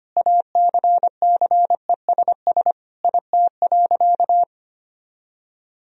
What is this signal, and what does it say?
Morse code, heard as a keyed tone, about 25 wpm, 700 Hz. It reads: ACCESS IT.
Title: Sentences from Top 1000 Words - Review Part 4 - 25wpm